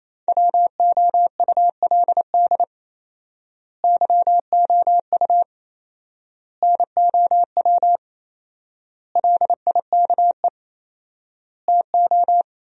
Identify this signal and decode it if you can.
Morse code, heard as a keyed tone, about 27 wpm, 700 Hz. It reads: WOULD YOU NOW LIKE TO
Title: 4x Word-Spacing - Sentences from Top 200 Words - 28wpm